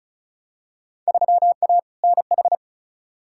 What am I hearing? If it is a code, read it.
Morse code, heard as a keyed tone, about 36 wpm, 700 Hz. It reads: 3A NH
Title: Field Day Exchanges - 35wpm